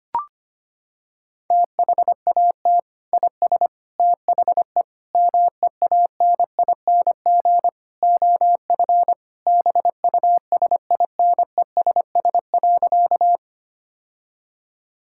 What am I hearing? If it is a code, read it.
Morse code, heard as a keyed tone, about 25 wpm, 700 Hz. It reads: THAT IS THE MEANING OF BUSINESS.
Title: Sentences from Top 1000 Words - Review Part 6 - 25wpm